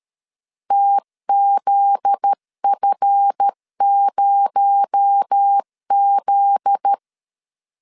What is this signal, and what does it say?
Morse code, heard as a keyed tone, about 13 wpm, 800 Hz. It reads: TZF0Z
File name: LEZIONI / 7 GF9Z / 7 GF9Z 12wpm 2147